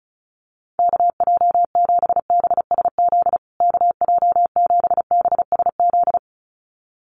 Morse code, heard as a keyed tone, about 34 wpm, 700 Hz. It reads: KJ7BSZ KJ7BSZ
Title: USA Call Signs Repeated 2x 35wpm